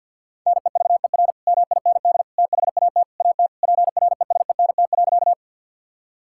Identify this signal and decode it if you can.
Morse code, heard as a keyed tone, about 50 wpm, 700 Hz. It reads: NEVER MIND THAT AT PRESENT.
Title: Sentences from Top 1000 Words - Review Part 5 - 50wpm